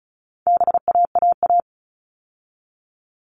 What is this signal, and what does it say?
Morse code, heard as a keyed tone, about 34 wpm, 700 Hz. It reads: BAAA
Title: All Cut Numbers - 4-Digit 35wpm